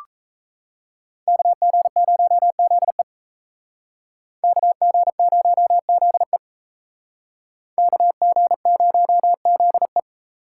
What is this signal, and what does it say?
Morse code, heard as a keyed tone, about 33 wpm, 700 Hz. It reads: KG0ZE KG0ZE KG0ZE
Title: Speed Racing - USA 2x2 Call Signs 28wpm